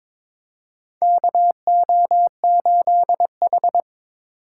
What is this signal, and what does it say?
Morse code, heard as a keyed tone, about 22 wpm, 700 Hz. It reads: KO8H
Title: USA Call Signs - 4 Characters - 3x Character Spacing - 22wpm